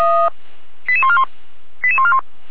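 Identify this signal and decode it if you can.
Five-tone selective call (ZVEI), heard as quick runs of five tones, tones of about 70 ms. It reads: 88141 88141
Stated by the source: Five-tone ZVEI selective call received by SDR, fire brigade Feuerwehr Orlamünde.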